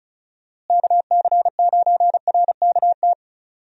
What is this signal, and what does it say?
Morse code, heard as a keyed tone, about 36 wpm, 700 Hz. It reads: KC9RKT
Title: USA Call Signs - 6 Characters - Repeated 2x - 35wpm